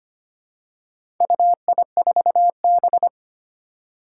Morse code, heard as a keyed tone, about 25 wpm, 700 Hz. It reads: UI4B
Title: Four Challenging Letter-Number Combinations 25wpm